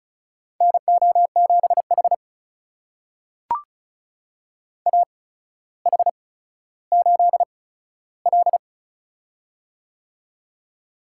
Morse code, heard as a keyed tone, about 36 wpm, spaced as for 15 wpm, 700 Hz. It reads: NO7H AH8L
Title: USA Call Signs - 4 Characters - 8x Character Spacing - 35wpm